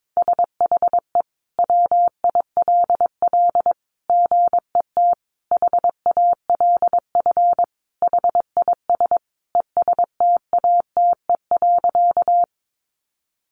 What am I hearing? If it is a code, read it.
Morse code, heard as a keyed tone, about 22 wpm, 700 Hz. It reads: SHE WILL GET HALF HIS ESTATE.
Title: Sentences from Top 1200 Words 22wpm